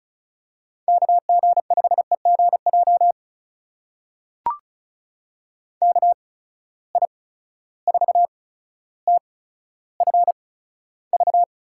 Morse code, heard as a keyed tone, about 36 wpm, spaced as for 15 wpm, 700 Hz. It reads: KG5EGJ KI4TFV
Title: USA Call Signs - 8x Character Spacing - 35wpm